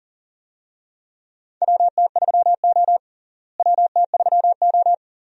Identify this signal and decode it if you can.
Morse code, heard as a keyed tone, about 37 wpm, 700 Hz. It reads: WT3O WT3O
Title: USA Call Signs - 4 Characters - Repeated 2x - 40wpm